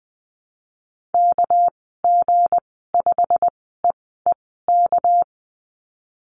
Morse code, heard as a keyed tone, about 20 wpm, spaced as for 15 wpm, 700 Hz. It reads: KG5EEK
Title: 200 Lesson 32g - Letter k - Call Signs 20wpm